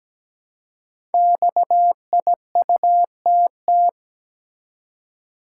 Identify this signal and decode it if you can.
Morse code, heard as a keyed tone, about 17 wpm, 700 Hz. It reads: XIUTT
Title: Code Groups - 5 Letters - 17wpm